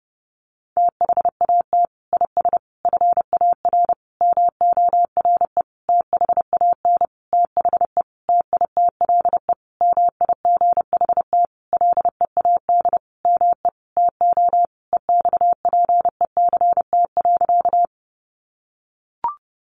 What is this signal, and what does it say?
Morse code, heard as a keyed tone, about 30 wpm, 700 Hz. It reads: THAT IS FAR MORE THAN THE TITLE MIGHT LEAD ME TO EXPECT.
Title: Sentences from Top 1200 Words 30wpm